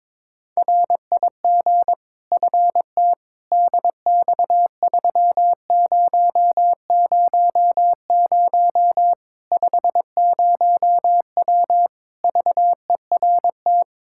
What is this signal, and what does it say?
Morse code, heard as a keyed tone, about 21 wpm, 700 Hz. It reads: RIG FT DX3000 50W VERT
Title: Practice QSO - Exchange 2 - Info - 22wpm